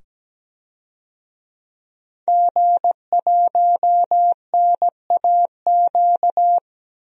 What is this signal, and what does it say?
Morse code, heard as a keyed tone, about 17 wpm, 700 Hz. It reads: G1NAQ
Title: DXCC-100 Call Signs - 5 Characters - 17wpm